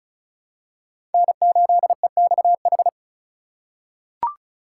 Morse code, heard as a keyed tone, about 34 wpm, 700 Hz. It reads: N8EXH
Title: USA Call Signs - 5 Characters - 3x Character Spacing - 35wpm